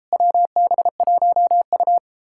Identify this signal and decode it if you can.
Morse code, heard as a keyed tone, about 38 wpm, 700 Hz. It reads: WB1U